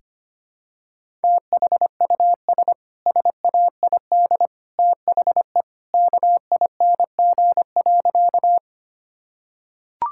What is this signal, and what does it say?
Morse code, heard as a keyed tone, about 25 wpm, 700 Hz. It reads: THUS SAID THE KING.
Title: Sentences from Top 1000 Words - Review Part 5 - 25wpm